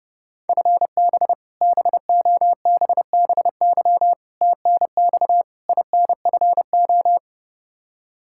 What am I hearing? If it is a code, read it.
Morse code, heard as a keyed tone, about 30 wpm, 700 Hz. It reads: FB BOBBY TNX INFO
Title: Practice QSO - Exchange 3 - More Info - 30wpm